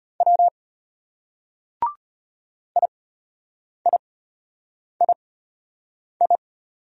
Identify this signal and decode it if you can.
Morse code, heard as a keyed tone, about 33 wpm, spaced as for 8 wpm, 700 Hz. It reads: W IIII